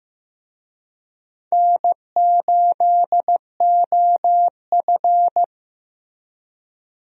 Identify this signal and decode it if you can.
Morse code, heard as a keyed tone, about 15 wpm, 700 Hz. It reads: N8OF
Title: USA Call Signs - 4 Characters - Repeated 3x - 15wpm